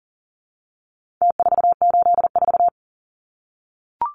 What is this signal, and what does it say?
Morse code, heard as a keyed tone, about 40 wpm, 700 Hz. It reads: T484
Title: Numbers - Contest Style - 4-Digit 40wpm